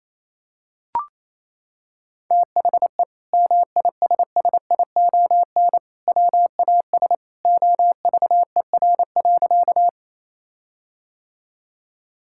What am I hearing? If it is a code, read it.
Morse code, heard as a keyed tone, about 28 wpm, 700 Hz. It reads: THE MISSION WAS OVER.
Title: Sentences from Top 1600 Words 28wpm